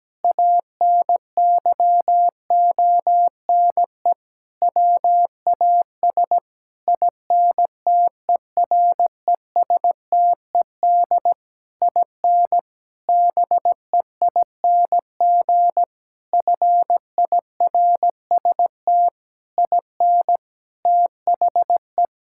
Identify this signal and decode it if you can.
Morse code, heard as a keyed tone, about 17 wpm, 700 Hz. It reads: ANYONE WAS INTERESTED IN BEING FIRST IN THE
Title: Sentences from Top 1600 Words 17wpm